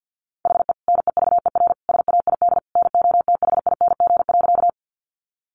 Morse code, heard as a keyed tone, about 50 wpm, 700 Hz. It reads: HE NEVER SAID NOTHING.
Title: Sentences from Top 500 Words 50wpm